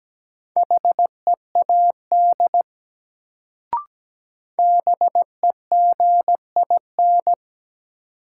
Morse code, heard as a keyed tone, about 17 wpm, 700 Hz. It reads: HEAD BEGIN
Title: Sentences from Top 300 Words - New Word Drill - 17wpm